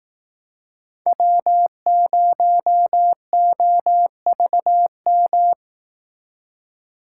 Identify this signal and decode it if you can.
Morse code, heard as a keyed tone, about 18 wpm, 700 Hz. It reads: W0OVM
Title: Speed Racing - USA 1x3 Call Signs 15wpm